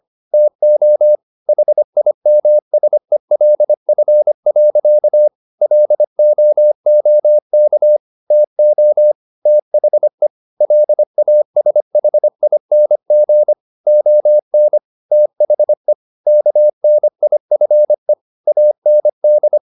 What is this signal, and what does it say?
Morse code, heard as a keyed tone, about 25 wpm, 600 Hz. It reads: TO HIMSELF. LOOK TO THE LASHING ON THE KNIFE AND